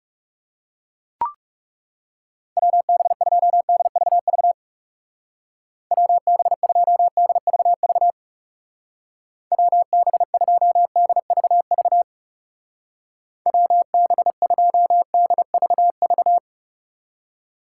Morse code, heard as a keyed tone, about 40 wpm, 700 Hz. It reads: WB2DVV WB2DVV WB2DVV WB2DVV